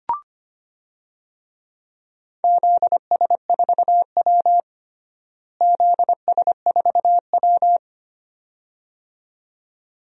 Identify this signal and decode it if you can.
Morse code, heard as a keyed tone, about 25 wpm, 700 Hz. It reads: ZS4W ZS4W
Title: DXCC-100 Call Signs - 4 Characters - Repeated 2x - 25wpm